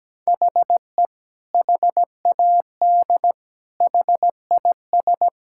Morse code, heard as a keyed tone, about 17 wpm, 700 Hz. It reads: HE HAD HIS
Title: Sentences from Top 1000 Words - Review Part 5 - 17wpm